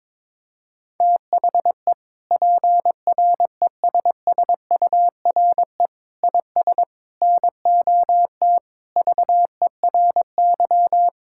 Morse code, heard as a keyed tone, about 22 wpm, 700 Hz. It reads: THE PRESSURE IS NOT VERY
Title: Sentences from Top 1000 Words - Review Part 6 - 22wpm